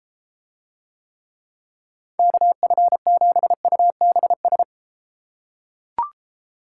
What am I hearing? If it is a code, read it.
Morse code, heard as a keyed tone, about 34 wpm, 700 Hz. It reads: KF7UBS